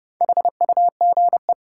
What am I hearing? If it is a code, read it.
Morse code, heard as a keyed tone, about 30 wpm, 700 Hz. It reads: HUGE
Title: Sentences from Top 1100 Words - New Word Drill - 30wpm